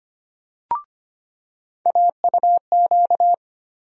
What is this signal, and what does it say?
Morse code, heard as a keyed tone, about 25 wpm, 700 Hz. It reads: AUQ
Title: USA Call Sign Postfixes - 3 Letters 25wpm